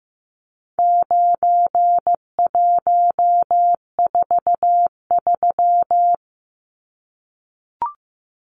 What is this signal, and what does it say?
Morse code, heard as a keyed tone, about 15 wpm, 700 Hz. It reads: 9143
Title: Numbers - 4-Digit 15wpm